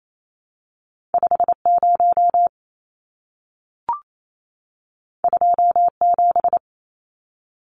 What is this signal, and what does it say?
Morse code, heard as a keyed tone, about 28 wpm, 700 Hz. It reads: 50 27
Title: Numbers - 2-Digit 28wpm